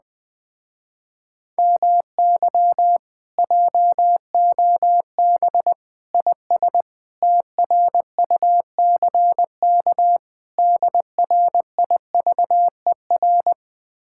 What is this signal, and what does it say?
Morse code, heard as a keyed tone, about 21 wpm, 700 Hz. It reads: MY JOB IS TRUCK DRIVER